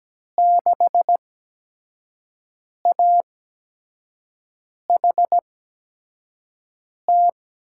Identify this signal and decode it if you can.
Morse code, heard as a keyed tone, about 17 wpm, spaced as for 5 wpm, 700 Hz. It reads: 6AHT
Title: USA Call Signs - 5 Characters - 8x Character Spacing - 17wpm